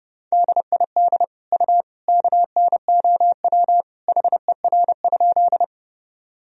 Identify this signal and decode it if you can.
Morse code, heard as a keyed tone, about 32 wpm, 700 Hz. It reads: DID U KNOW HER?
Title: Sentences from Top 100 Words and Abbreviations 30wpm